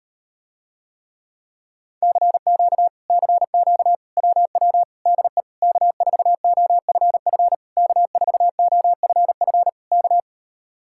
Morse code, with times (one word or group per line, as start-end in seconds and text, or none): 2.0-2.9 CQ
3.1-3.9 CQ
4.2-4.8 WW
5.1-5.4 DE
5.6-7.5 K4OFF
7.8-9.7 K4OFF
9.9-10.2 K